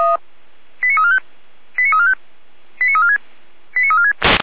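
Five-tone selective call (ZVEI), heard as quick runs of five tones, tones of about 80 ms. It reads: 89345 89345 89346 89346
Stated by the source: Five-tone ZVEI selective call received by SDR, fire brigade Feuerwehr Orlamünde.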